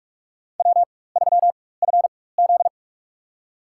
Morse code, heard as a keyed tone, about 46 wpm, spaced as for 21 wpm, 700 Hz. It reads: W3FZ
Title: USA Call Signs - 4x Character Spacing - 45wpm